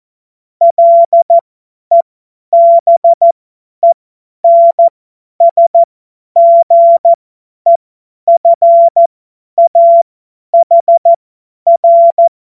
Morse code, heard as a keyed tone, about 14 wpm, spaced as for 10 wpm, 680 Hz. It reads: LEBENSGEFAHR
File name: Morsekurs / Witze / Witz 06. Dezember 2022 50 F70